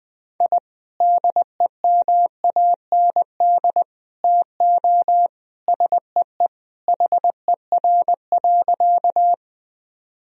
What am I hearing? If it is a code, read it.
Morse code, heard as a keyed tone, about 20 wpm, 700 Hz. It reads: I DEMAND TO SEE HER.
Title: Sentences from Top 1000 Words - Review Part 4 - 20wpm